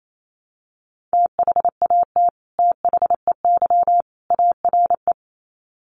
Morse code, heard as a keyed tone, about 28 wpm, 700 Hz. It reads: THAT THEY ARE
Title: Sets of 3 Words 28wpm